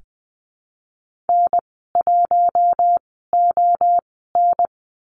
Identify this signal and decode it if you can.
Morse code, heard as a keyed tone, about 20 wpm, spaced as for 14 wpm, 700 Hz. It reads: N1ON
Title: 042 Lesson 8g - Number 1 - Call Signs 20wpm